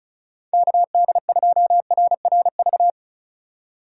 Morse code, heard as a keyed tone, about 36 wpm, 700 Hz. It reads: KD2RRV
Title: USA Call Signs - 6 Characters - 2x Character Spacing - 35wpm